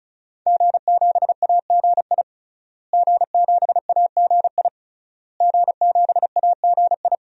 Morse code, heard as a keyed tone, about 34 wpm, 700 Hz. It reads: G7AGI G7AGI G7AGI
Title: DXCC-100 Call Signs - 5 Characters - Repeated 3x - 35wpm